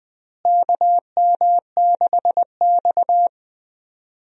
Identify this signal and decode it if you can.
Morse code, heard as a keyed tone, about 20 wpm, 700 Hz. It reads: KM6X
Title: USA Call Signs - 4 Characters - Repeated 3x - 20wpm